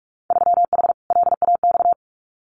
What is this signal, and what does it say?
Morse code, heard as a keyed tone, about 46 wpm, 700 Hz. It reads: 3H LAX
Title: Winter Field Day Exchanges 45wpm